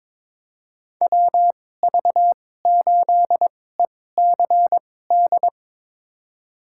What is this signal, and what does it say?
Morse code, heard as a keyed tone, about 22 wpm, spaced as for 16 wpm, 700 Hz. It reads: WV8ECD
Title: USA Call Signs - 2x Character Spacing - 22wpm